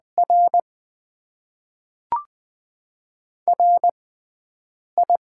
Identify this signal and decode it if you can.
Morse code, heard as a keyed tone, about 20 wpm, spaced as for 3 wpm, 700 Hz. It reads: RR I
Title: Top 20 QSO Elements - 6x Character Spacing - 20wpm